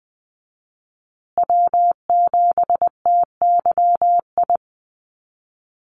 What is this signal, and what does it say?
Morse code, heard as a keyed tone, about 20 wpm, 700 Hz. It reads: W7TYI
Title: USA Call Signs Repeated 2x 20wpm